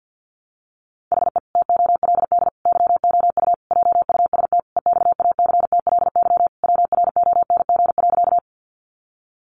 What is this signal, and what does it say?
Morse code, heard as a keyed tone, about 50 wpm, 700 Hz. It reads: HE TOLD YOU JUST EXACTLY WRONG.
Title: Sentences from Top 900 Words 50wpm